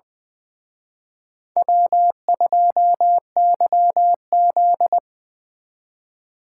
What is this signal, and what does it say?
Morse code, heard as a keyed tone, about 25 wpm, 700 Hz. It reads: W2YZ
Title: Speed Racing - USA Call Signs 20wpm